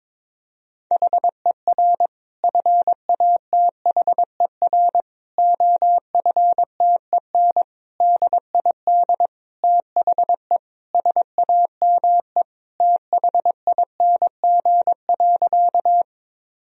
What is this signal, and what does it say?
Morse code, heard as a keyed tone, about 22 wpm, 700 Hz. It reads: HER FATHER OFTEN DID THE SAME THING.
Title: Sentences from Top 1000 Words - Review Part 1 - 22wpm